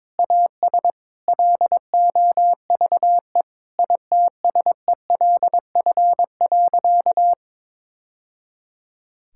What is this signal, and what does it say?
Morse code, heard as a keyed tone, about 22 wpm, 700 Hz. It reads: AS LOVE ITSELF.
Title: Sentences from Top 1000 Words - Review Part 5 - 22wpm